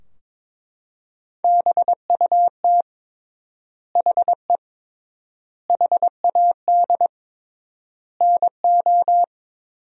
Morse code, heard as a keyed tone, about 22 wpm, 700 Hz. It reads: BUT HE HAD NO
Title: 3x Word-Spacing - Sentences from Top 100 Words - 22wpm